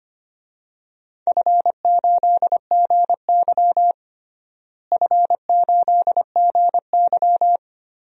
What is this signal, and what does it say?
Morse code, heard as a keyed tone, about 25 wpm, 700 Hz. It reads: F8GY F8GY
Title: DXCC-100 Call Signs - 4 Characters - Repeated 2x - 25wpm